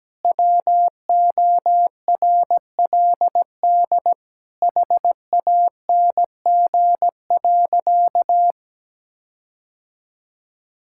Morse code, heard as a keyed tone, about 17 wpm, 700 Hz. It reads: WORLD HANG.
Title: Sentences from Top 1600 Words 17wpm